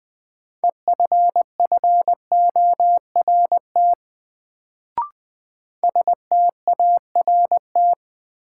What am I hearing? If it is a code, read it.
Morse code, heard as a keyed tone, about 20 wpm, 700 Hz. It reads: EFFORT START
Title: Sets of 4 Words - New Word Drill - 20wpm